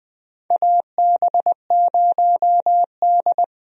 Morse code, heard as a keyed tone, about 20 wpm, 700 Hz. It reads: AB0D
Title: USA Call Signs - 4 Characters - Repeated 2x - 20wpm